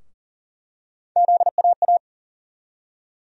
Morse code, heard as a keyed tone, about 40 wpm, 700 Hz. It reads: ZAA
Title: USA Call Sign Postfixes - 3 Letters 40wpm